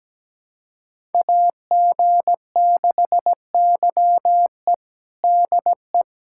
Morse code, heard as a keyed tone, about 17 wpm, 700 Hz. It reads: AG6YE DE
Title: USA Callsign DE Callsign 17wpm